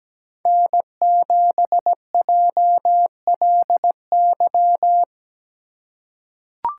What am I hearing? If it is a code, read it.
Morse code, heard as a keyed tone, about 17 wpm, 700 Hz. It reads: N7JLY
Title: USA Call Signs - 5 Characters - 3x Character Spacing - 17wpm